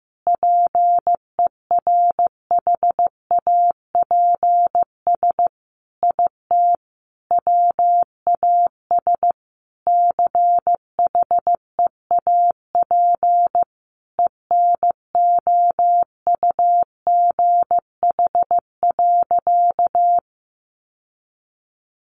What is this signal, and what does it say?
Morse code, heard as a keyed tone, about 15 wpm, 700 Hz. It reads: PERHAPS IT WAS CHEAP ENOUGH.
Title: Sentences from Top 1200 Words 15wpm